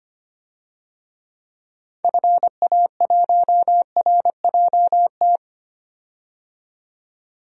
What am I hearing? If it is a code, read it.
Morse code, heard as a keyed tone, about 25 wpm, 700 Hz. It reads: FA1RJT
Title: DXCC-100 Call Signs - 6 Characters - 25wpm